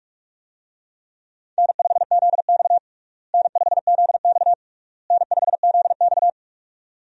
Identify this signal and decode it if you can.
Morse code, heard as a keyed tone, about 41 wpm, 700 Hz. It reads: N5ZX N5ZX N5ZX
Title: USA Call Signs - 4 Characters - Repeated 3x - 45wpm